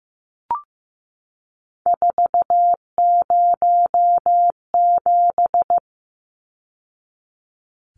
Morse code, heard as a keyed tone, about 15 wpm, 700 Hz. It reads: 407
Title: Numbers - 3-Digit 15wpm